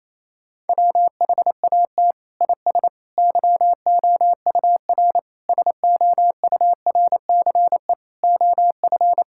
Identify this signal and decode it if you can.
Morse code, heard as a keyed tone, about 27 wpm, 700 Hz. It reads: WHAT IS YOUR SOURCE OF
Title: Sentences from Top 1000 Words - Review Part 5 - 28wpm